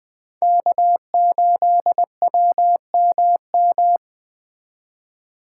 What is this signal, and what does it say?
Morse code, heard as a keyed tone, about 20 wpm, 700 Hz. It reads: K8WMM